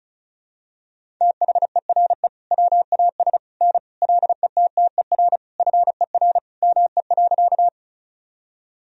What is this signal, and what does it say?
Morse code, heard as a keyed tone, about 36 wpm, 700 Hz. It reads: THERE WAS N LETTER FER ME.
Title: Sentences from Top 300 Words and Abbreviations 35wpm